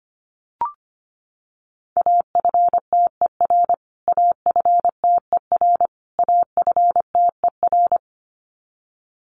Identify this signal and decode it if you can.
Morse code, heard as a keyed tone, about 25 wpm, 700 Hz. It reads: AFTER AFTER AFTER
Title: Top 500 Words Repeated 25wpm